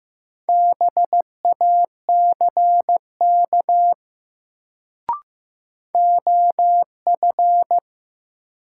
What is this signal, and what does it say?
Morse code, heard as a keyed tone, about 15 wpm, 700 Hz. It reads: BACK OF
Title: Sets of 2 Words - New Word Drill - 15wpm